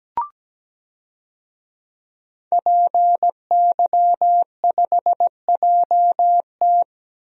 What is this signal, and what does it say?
Morse code, heard as a keyed tone, about 17 wpm, 700 Hz. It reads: PY5JT
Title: DXCC-100 Call Signs - 5 Characters - 17wpm